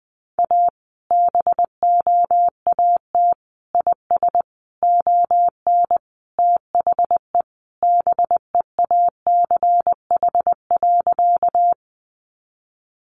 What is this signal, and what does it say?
Morse code, heard as a keyed tone, about 20 wpm, 700 Hz. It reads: A BOAT IS ON THE BEACH.
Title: Sentences from Top 1500 Words 20wpm